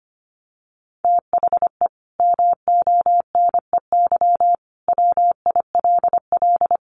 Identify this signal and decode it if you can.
Morse code, heard as a keyed tone, about 25 wpm, 700 Hz. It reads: THE MONEY WILL
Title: Sentences from Top 1500 Words 25wpm